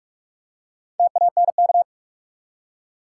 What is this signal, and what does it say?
Morse code, heard as a keyed tone, about 47 wpm, 700 Hz. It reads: TANK
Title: Four Letter Words - 45wpm